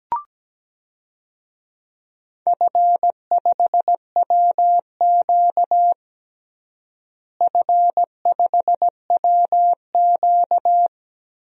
Morse code, heard as a keyed tone, about 17 wpm, 700 Hz. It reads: F5WQ F5WQ
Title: DXCC-100 Call Signs - 4 Characters - Repeated 2x - 17wpm